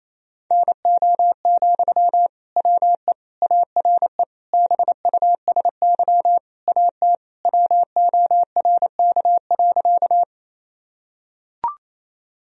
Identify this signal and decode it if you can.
Morse code, heard as a keyed tone, about 28 wpm, 700 Hz. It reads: NO, WE ARE BUSY AT WORK.
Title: Sentences from Top 1000 Words - Review Part 3 - 28wpm